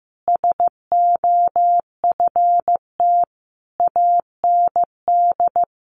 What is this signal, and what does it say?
Morse code, heard as a keyed tone, about 15 wpm, 700 Hz. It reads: SOFT AND
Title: Sentences from Top 1500 Words 15wpm